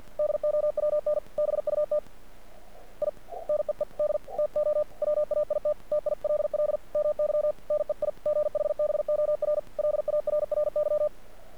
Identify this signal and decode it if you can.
Morse code, heard as a keyed tone, about 48 wpm, 600 Hz. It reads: DOWN BUT I NEED TO WAIT TILL MY NEIGHBOR LARRY
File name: archived / CRSnet